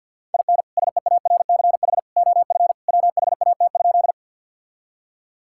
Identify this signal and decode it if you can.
Morse code, heard as a keyed tone, about 51 wpm, 700 Hz. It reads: IN SEARCH OF WHAT?
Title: Sentences from Top 1000 Words - Review Part 5 - 50wpm